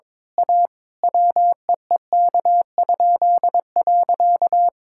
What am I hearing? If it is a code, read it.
Morse code, heard as a keyed tone, about 21 wpm, 700 Hz. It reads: A WEEK?.
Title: Pangram Challenge v2 - 22wpm